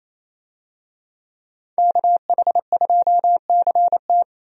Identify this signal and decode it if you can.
Morse code, heard as a keyed tone, about 27 wpm, 700 Hz. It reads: KH2CT
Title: USA Call Signs - 5 Characters - 28wpm